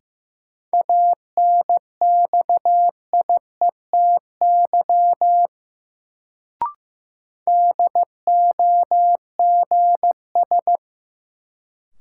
Morse code, heard as a keyed tone, about 15 wpm, 700 Hz. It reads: ANXIETY DOGS
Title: Sentences from Top 1600 Words - New Word Drill Alt - 15wpm